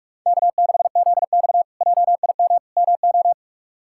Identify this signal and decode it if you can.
Morse code, heard as a keyed tone, about 46 wpm, 700 Hz. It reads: K6ZX JIM MO
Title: K1USN SST QSO Practice - 45wpm